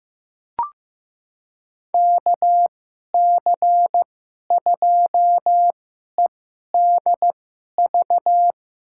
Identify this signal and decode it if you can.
Morse code, heard as a keyed tone, about 15 wpm, spaced as for 11 wpm, 700 Hz. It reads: KC2EDV
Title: USA Call Signs - 2x Character Spacing - 15wpm